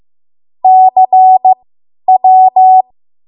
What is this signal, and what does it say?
Morse code, heard as a keyed tone, about 15 wpm, spaced as for 10 wpm, 750 Hz. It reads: CW
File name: Morse / 10WPM